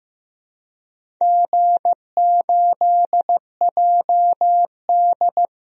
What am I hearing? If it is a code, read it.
Morse code, heard as a keyed tone, about 15 wpm, 700 Hz. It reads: G8JD
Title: DXCC-100 Call Signs - 4 Characters - Repeated 2x - 15wpm